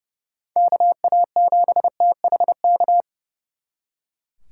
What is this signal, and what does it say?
Morse code, heard as a keyed tone, about 38 wpm, 700 Hz. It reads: KA7THK
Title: Speed Racing - USA 2x3 Call Signs 30wpm